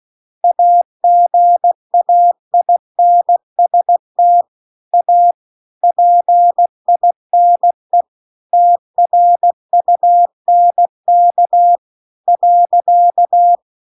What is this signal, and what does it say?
Morse code, heard as a keyed tone, about 16 wpm, 700 Hz. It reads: AGAINST A PINE TRUNK .
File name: CW-The-Big-Twoheaded-River-2-16WPM